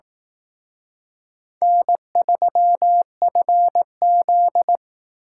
Morse code, heard as a keyed tone, about 18 wpm, 700 Hz. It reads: N3FZ
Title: Speed Racing - USA 1x2 Call Signs 15wpm